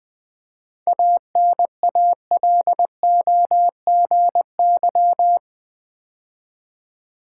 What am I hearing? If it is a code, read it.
Morse code, heard as a keyed tone, about 20 wpm, 690 Hz. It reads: ANALOGY
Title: 7 Letter Words - First 3 Letters Given - 20wpm